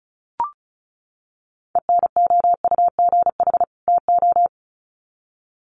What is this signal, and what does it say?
Morse code, heard as a keyed tone, about 34 wpm, 700 Hz. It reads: ENOUGH TO
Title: Sets of 2 Words 35wpm